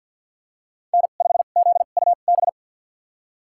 Morse code, heard as a keyed tone, about 50 wpm, spaced as for 36 wpm, 700 Hz. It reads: N5ZUB
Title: USA Call Signs - 2x Character Spacing - 50wpm